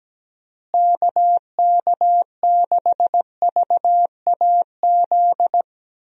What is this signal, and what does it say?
Morse code, heard as a keyed tone, about 17 wpm, 700 Hz. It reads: KK6VAZ
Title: USA Call Signs - 6 Characters - Repeated 3x - 17wpm